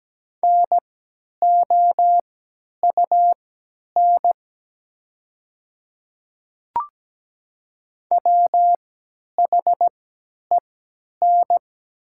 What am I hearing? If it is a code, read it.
Morse code, heard as a keyed tone, about 17 wpm, spaced as for 10 wpm, 700 Hz. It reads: NOUN WHEN
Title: 3x Character Spacing - Top 500 Words - 17wpm